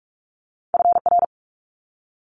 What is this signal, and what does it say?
Morse code, heard as a keyed tone, about 46 wpm, 700 Hz. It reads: FR
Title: Multi-letter Phonemes Quick Fire 45wpm